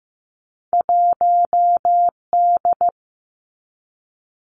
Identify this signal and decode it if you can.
Morse code, heard as a keyed tone, about 15 wpm, 700 Hz. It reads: 1D
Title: Two Challenging Letter-Number Combinations 15wpm